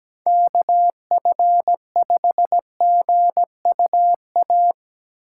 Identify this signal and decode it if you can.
Morse code, heard as a keyed tone, about 20 wpm, 700 Hz. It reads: KF5GUA
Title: Speed Racing - USA Call Signs 17wpm